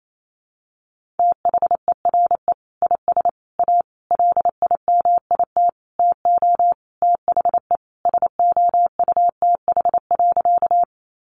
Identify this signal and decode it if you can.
Morse code, heard as a keyed tone, about 28 wpm, 700 Hz. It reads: THERE IS A LIMIT TO THE SOUTH.
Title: Sentences from Top 1500 Words 28wpm